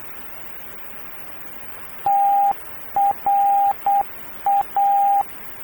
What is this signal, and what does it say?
Morse code, heard as a keyed tone, about 8 wpm, 800 Hz. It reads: TRA
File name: radio tv historisch / AM Sender / TRA